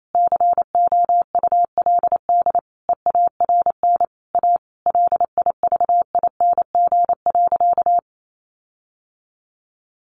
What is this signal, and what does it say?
Morse code, heard as a keyed tone, about 28 wpm, 700 Hz. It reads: COULD EARN A LIVING.
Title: Sentences from Top 1500 Words 28wpm